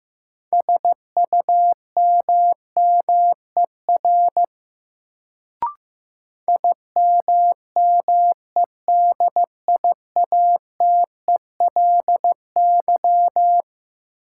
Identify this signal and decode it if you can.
Morse code, heard as a keyed tone, about 15 wpm, 700 Hz. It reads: SUMMER IMMEDIATELY
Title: Sentences from Top 1100 Words - New Word Drill Alt - 15wpm